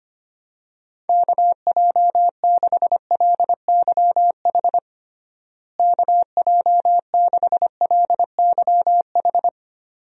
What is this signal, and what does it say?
Morse code, heard as a keyed tone, about 25 wpm, 700 Hz. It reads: KJ6LYH KJ6LYH
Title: USA Call Signs - 6 Characters - Repeated 2x - 25wpm